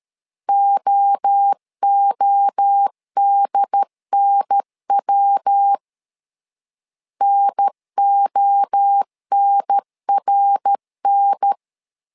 Morse code, heard as a keyed tone, about 13 wpm, 800 Hz. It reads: OODNW NONRN
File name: LEZIONI / 3 MWRO / 3 MWRO 12wpm 1730